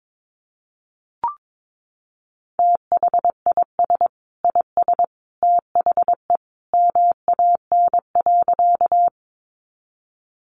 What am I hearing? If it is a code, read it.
Morse code, heard as a keyed tone, about 22 wpm, 700 Hz. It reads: THIS IS THE MAN.
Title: Sentences from Top 200 Words 22wpm